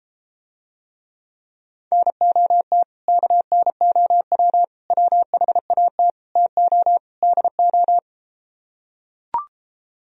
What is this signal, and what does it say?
Morse code, heard as a keyed tone, about 33 wpm, 700 Hz. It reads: NOT KNOW WHAT TO DO